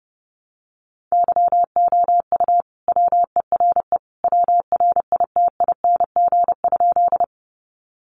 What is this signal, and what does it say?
Morse code, heard as a keyed tone, about 28 wpm, 700 Hz. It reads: YOU WERE WRITING?
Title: Sentences from Top 800 Words 30wpm